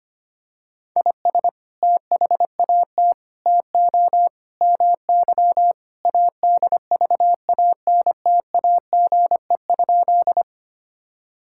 Morse code, heard as a keyed tone, about 25 wpm, 700 Hz. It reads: IS THAT TO MY ADVANTAGE?
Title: Sentences from Top 1000 Words - Review Part 6 - 25wpm